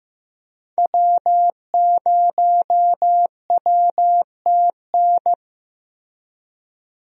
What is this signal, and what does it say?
Morse code, heard as a keyed tone, about 15 wpm, 700 Hz. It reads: W0WTN
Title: USA Call Signs - 5 Characters - Repeated 3x - 15wpm